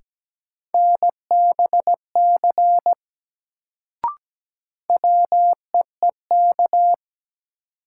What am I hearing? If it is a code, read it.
Morse code, heard as a keyed tone, about 17 wpm, 700 Hz. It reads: NBC WEEK
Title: Sets of 5 Words Encore - New Word Drill - 17wpm